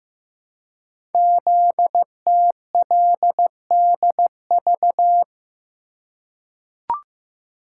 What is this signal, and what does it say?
Morse code, heard as a keyed tone, about 15 wpm, 700 Hz. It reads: ZTLDV
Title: Code Groups - 5 Letters - 15wpm